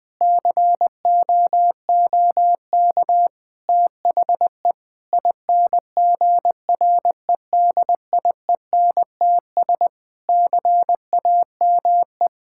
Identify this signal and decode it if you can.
Morse code, heard as a keyed tone, about 20 wpm, 700 Hz. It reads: COOK THE INGREDIENTS CAME
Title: Sentences with 5-Word Phrases - 20wpm